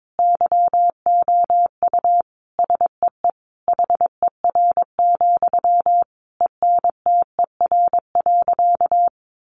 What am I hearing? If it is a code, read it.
Morse code, heard as a keyed tone, about 22 wpm, 700 Hz. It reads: YOU SEE HER, ENTER.
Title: Sentences from Top 1300 Words 22wpm